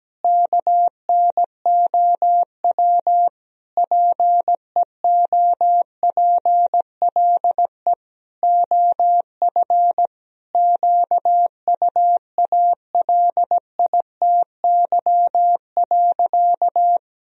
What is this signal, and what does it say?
Morse code, heard as a keyed tone, about 17 wpm, 700 Hz. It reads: KNOW PEOPLE OF QUALITY.
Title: Sentences from Top 1000 Words - Review Part 4 - 17wpm